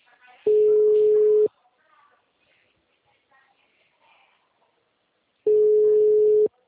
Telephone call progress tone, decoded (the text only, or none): ringback tone